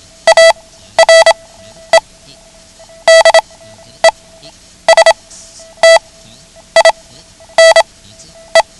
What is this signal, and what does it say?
Morse code, heard as a keyed tone, about 27 wpm, spaced as for 10 wpm, 660 Hz. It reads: ARE DESTINE
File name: ham radio / CodeQuick / Code Quick 09